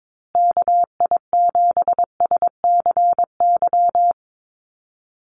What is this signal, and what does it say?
Morse code, heard as a keyed tone, about 22 wpm, 700 Hz. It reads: KI7SCY
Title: USA Call Signs Repeated 3x 22wpm